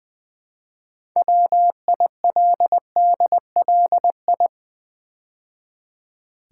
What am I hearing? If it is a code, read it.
Morse code, heard as a keyed tone, about 20 wpm, 700 Hz. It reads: WILDLI
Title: Build A Word - 2 Letters At A Time - 8-Letter Words - 20wpm